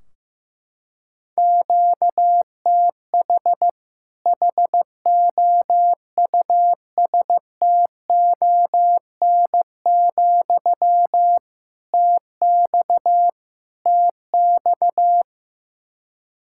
Morse code, heard as a keyed tone, about 15 wpm, 700 Hz. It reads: QTH HOUSTON, TX TX
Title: Practice QSO - Exchange 1 - Beginning - 15wpm